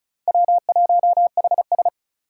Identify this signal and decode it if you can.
Morse code, heard as a keyed tone, about 36 wpm, 700 Hz. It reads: W1HS
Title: USA Call Signs - 4 Characters - 3x Character Spacing - 35wpm